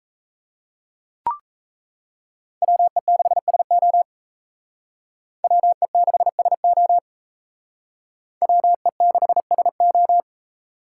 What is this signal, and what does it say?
Morse code, heard as a keyed tone, about 36 wpm, 700 Hz. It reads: WE6SO WE6SO WE6SO